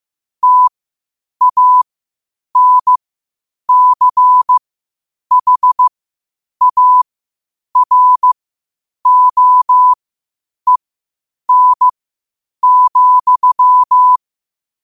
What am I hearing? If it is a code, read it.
Morse code, heard as a keyed tone, about 15 wpm, spaced as for 8 wpm, 1000 Hz. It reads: TANCHAROEN,
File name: qotdfiles / 8wpm